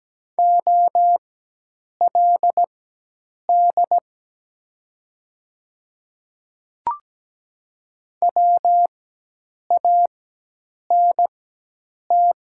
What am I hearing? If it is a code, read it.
Morse code, heard as a keyed tone, about 17 wpm, spaced as for 8 wpm, 700 Hz. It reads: OLD WANT
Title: 4x Character Spacing - Top 500 Words - 17wpm